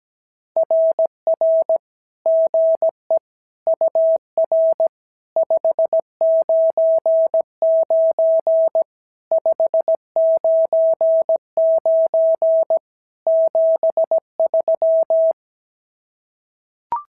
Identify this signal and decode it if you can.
Morse code, heard as a keyed tone, about 17 wpm, 650 Hz. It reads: RR GE UR 599 599 73
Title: SOTA QSO Practice 17wpm